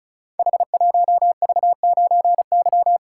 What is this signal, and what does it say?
Morse code, heard as a keyed tone, about 36 wpm, 700 Hz. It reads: H1V9Y
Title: Code Groups - 5 Letter:Numbers - 35wpm